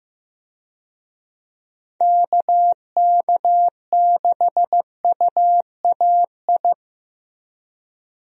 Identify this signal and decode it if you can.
Morse code, heard as a keyed tone, about 15 wpm, 700 Hz. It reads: KK6UAI
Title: USA Call Signs - 6 Characters - Repeated 3x - 15wpm